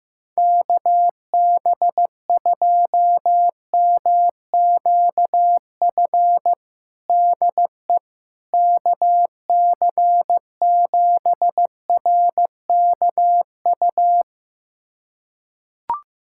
Morse code, heard as a keyed tone, about 15 wpm, 700 Hz. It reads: KB2MQF DE KC7RKU
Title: USA Callsign DE Callsign 15wpm